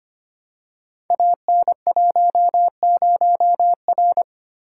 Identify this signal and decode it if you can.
Morse code, heard as a keyed tone, about 25 wpm, 700 Hz. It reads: AN10R
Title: Code Groups - 5 Letter:Numbers - 25wpm